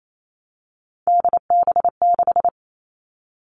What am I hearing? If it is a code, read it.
Morse code, heard as a keyed tone, about 28 wpm, 700 Hz. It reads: DB6
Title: All Cut Numbers - 3-Digit 28wpm